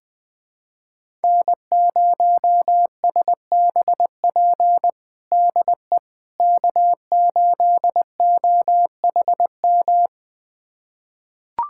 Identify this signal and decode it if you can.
Morse code, heard as a keyed tone, about 20 wpm, 700 Hz. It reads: N0SBP DE K8OHM
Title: USA Callsign DE Callsign - 5 Characters - 20wpm